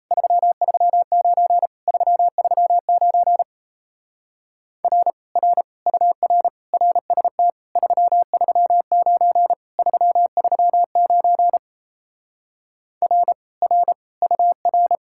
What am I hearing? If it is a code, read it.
Morse code, heard as a keyed tone, about 38 wpm, 700 Hz. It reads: 339 339 R R UR RST 339 339 R R UR